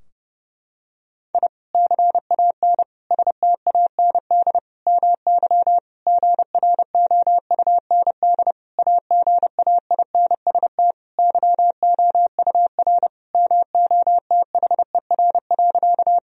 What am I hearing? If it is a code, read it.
Morse code, heard as a keyed tone, about 30 wpm, 700 Hz. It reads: I CAN STAND MY GROUND AGAINST YOUR MOTHER.
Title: Sentences from Top 1000 Words - Review Part 2 - 30wpm